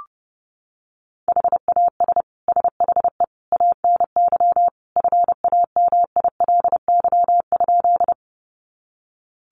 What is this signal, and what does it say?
Morse code, heard as a keyed tone, about 30 wpm, 700 Hz. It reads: HAS SHE ANY FAMILY?
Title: Sentences from Top 400 Words 30wpm